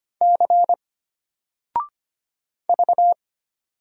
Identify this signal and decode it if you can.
Morse code, heard as a keyed tone, about 25 wpm, 700 Hz. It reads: C V
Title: Single Character (Letter:Number:Punctuation:Prosign) Alt 25wpm